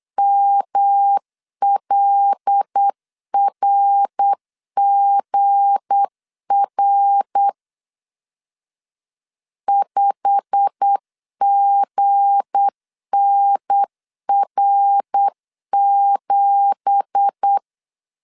Morse code, heard as a keyed tone, about 8 wpm, 800 Hz. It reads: MLRGR 5GNR7
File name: CW / DET / 1 9 1500C S A 8wpm 5637